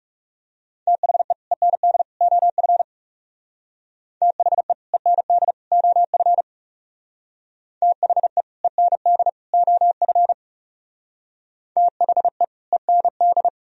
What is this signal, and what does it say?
Morse code, heard as a keyed tone, about 40 wpm, 700 Hz. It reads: THE END OF THE END OF THE END OF THE END